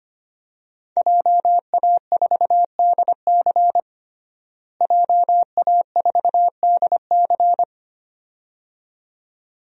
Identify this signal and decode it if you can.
Morse code, heard as a keyed tone, about 25 wpm, 700 Hz. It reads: JA4DC JA4DC
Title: DXCC-100 Call Signs - 5 Characters - Repeated 2x - 25wpm